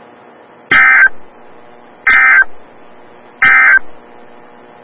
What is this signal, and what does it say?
Emergency Alert System (SAME) tones, heard as harsh buzzing bursts. It is heard from 0.6 to 3.9 s.
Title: Des Moines, IA NOAA Weather Radio - WXL57